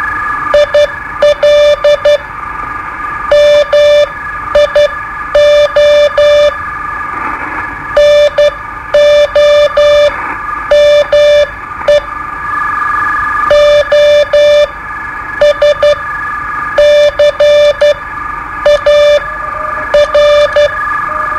Morse code, heard as a keyed tone, about 12 wpm, spaced as for 9 wpm, 590 Hz. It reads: IL MIO NOME OSCAR
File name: qso-01